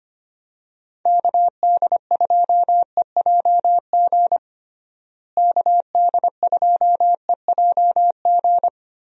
Morse code, heard as a keyed tone, about 25 wpm, 700 Hz. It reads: KD2EJG KD2EJG
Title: USA Call Signs - 6 Characters - Repeated 3x - 25wpm